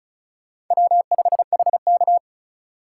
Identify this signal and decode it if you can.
Morse code, heard as a keyed tone, about 45 wpm, 700 Hz. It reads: W5HK